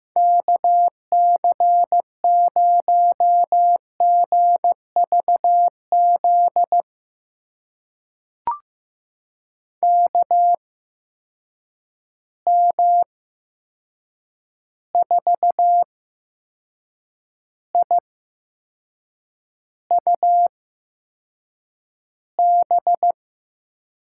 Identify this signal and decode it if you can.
Morse code, heard as a keyed tone, about 15 wpm, spaced as for 6 wpm, 700 Hz. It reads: KC0GVZ KM4IUB
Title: USA Call Signs - 6 Characters - 8x Character Spacing - 15wpm